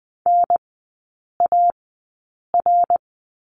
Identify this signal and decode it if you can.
Morse code, heard as a keyed tone, about 20 wpm, spaced as for 8 wpm, 700 Hz. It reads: NAR
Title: 053 Lesson 10e - Letter r - Three Characters 20wpm